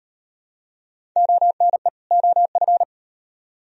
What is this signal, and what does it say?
Morse code, heard as a keyed tone, about 38 wpm, 700 Hz. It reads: ONE OF